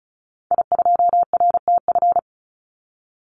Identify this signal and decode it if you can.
Morse code, heard as a keyed tone, about 34 wpm, 700 Hz. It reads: I2RTF
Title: Call Signs - 35wpm